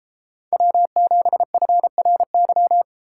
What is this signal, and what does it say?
Morse code, heard as a keyed tone, about 32 wpm, 700 Hz. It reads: W7FRY